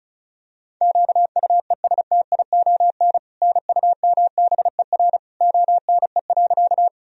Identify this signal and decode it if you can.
Morse code, heard as a keyed tone, about 36 wpm, 700 Hz. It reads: QUESTION NUMBER ONE.
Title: Sentences from Top 1000 Words - Review Part 1 - 35wpm